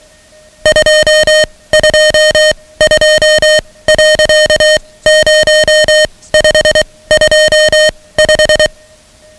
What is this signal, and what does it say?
Morse code, heard as a keyed tone, about 23 wpm, spaced as for 17 wpm, 620 Hz, with noise. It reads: 222.0525
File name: ham radio / CodeQuick / Code Quick 10